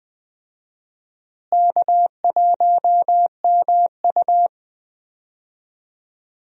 Morse code, heard as a keyed tone, about 21 wpm, 700 Hz. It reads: K1MU
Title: USA Call Signs - 4 Characters - Repeated 3x - 20wpm